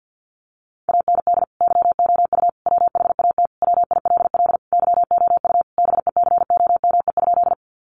Morse code, heard as a keyed tone, about 50 wpm, 700 Hz. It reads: AND YOU WHAT WILL YOU BECOME?
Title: Sentences from Top 600 Words 50wpm